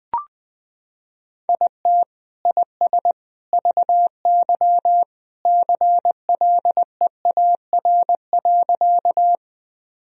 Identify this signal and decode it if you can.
Morse code, heard as a keyed tone, about 21 wpm, 700 Hz. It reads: IT IS VY CLEAR.
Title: Sentences from Top 500 Words and Abbreviations 20wpm